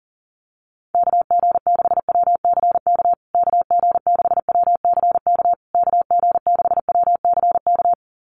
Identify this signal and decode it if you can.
Morse code, heard as a keyed tone, about 40 wpm, 700 Hz. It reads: KG6WCK KG6WCK KG6WCK
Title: USA Call Signs Repeated 3x 40wpm